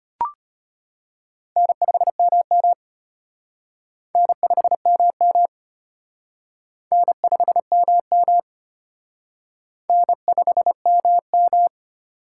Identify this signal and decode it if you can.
Morse code, heard as a keyed tone, about 30 wpm, 700 Hz. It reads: N5MM N5MM N5MM N5MM
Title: Speed Racing - USA 1x2 Call Signs 25wpm